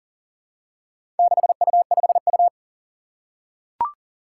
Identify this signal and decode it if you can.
Morse code, heard as a keyed tone, about 38 wpm, 700 Hz. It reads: 6U5U
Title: Four Challenging Letter-Number Combinations 40wpm